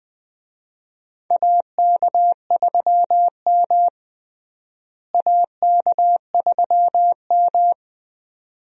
Morse code, heard as a keyed tone, about 21 wpm, 700 Hz. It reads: AK3M AK3M
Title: USA Call Signs - 4 Characters - Repeated 2x - 20wpm